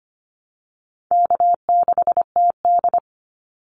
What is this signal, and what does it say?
Morse code, heard as a keyed tone, about 25 wpm, 700 Hz. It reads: K6TD
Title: Easy Call Signs - 25wpm